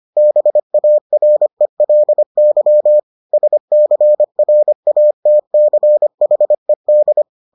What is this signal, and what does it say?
Morse code, heard as a keyed tone, about 25 wpm, 600 Hz. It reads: BARELY SCRATCHED